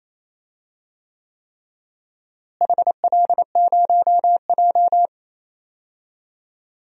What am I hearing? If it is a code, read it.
Morse code, heard as a keyed tone, about 28 wpm, 700 Hz. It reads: HL0J
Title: DXCC-100 Call Signs - 4 Characters - Repeated 3x - 28wpm